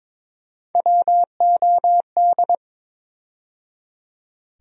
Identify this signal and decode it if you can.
Morse code, heard as a keyed tone, about 22 wpm, 700 Hz. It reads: WOD
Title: USA Call Sign Postfixes - 3 Letters 22wpm